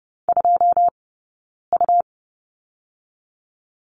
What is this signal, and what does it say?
Morse code, heard as a keyed tone, about 30 wpm, 700 Hz. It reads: 2 U
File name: Lesson 18d - Letter m - Two Characters 30wpm